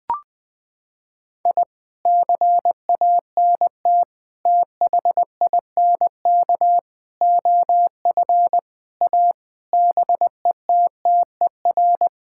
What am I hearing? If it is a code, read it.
Morse code, heard as a keyed tone, about 21 wpm, 700 Hz. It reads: I CANT THINK OF A BETTER
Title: Sets of 8 Words - 20wpm